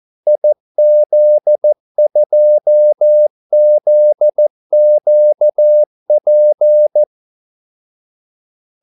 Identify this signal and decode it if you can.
Morse code, heard as a keyed tone, about 14 wpm, 600 Hz. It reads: IZ2ZQP
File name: Callsigns-14wpm